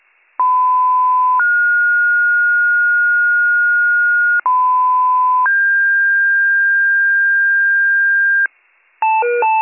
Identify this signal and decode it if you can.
Two-tone page, heard as a one-second tone, then a three-second tone, about 1010 then 1430 Hz, about 1010 then 1600 Hz, with noise. It is heard from 0.4 to 8.5 s.